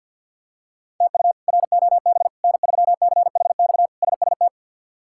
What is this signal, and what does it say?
Morse code, heard as a keyed tone, about 47 wpm, 700 Hz. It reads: TU ROB N3CHX SST
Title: K1USN SST QSO Practice - 50wpm